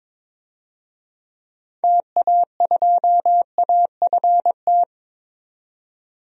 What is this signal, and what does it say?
Morse code, heard as a keyed tone, about 21 wpm, 700 Hz. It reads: TA2AFT
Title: DXCC-100 Call Signs - 6 Characters - 22wpm